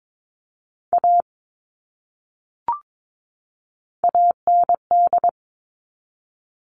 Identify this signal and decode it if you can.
Morse code, heard as a keyed tone, about 22 wpm, 700 Hz. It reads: A AND
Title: Top 500 Words 22wpm